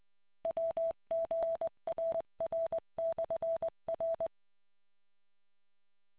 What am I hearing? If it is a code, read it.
Morse code, heard as a keyed tone, about 20 wpm, 660 Hz. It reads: WGRR/R